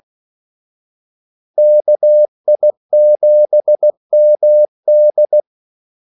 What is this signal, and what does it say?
Morse code, heard as a keyed tone, about 16 wpm, 600 Hz. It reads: KI7MD
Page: Kaldesignaler 16 wpm